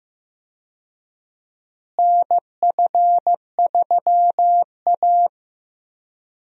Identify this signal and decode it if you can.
Morse code, heard as a keyed tone, about 15 wpm, 700 Hz. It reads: NF3A
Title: USA Call Signs - 4 Characters - Repeated 2x - 15wpm